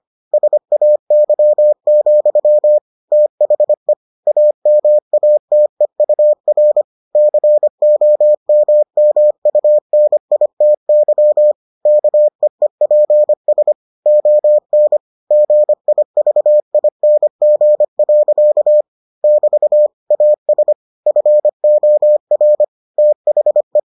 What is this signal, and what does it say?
Morse code, heard as a keyed tone, about 25 wpm, 600 Hz. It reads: SAY, THE AMATEUR COMMUNITY KEEPS ON GIVING. <BT> AS FOR THE